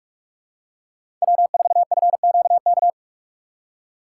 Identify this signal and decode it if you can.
Morse code, heard as a keyed tone, about 46 wpm, 700 Hz. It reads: W4FQK
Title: USA Call Signs - 5 Characters - 2x Character Spacing - 45wpm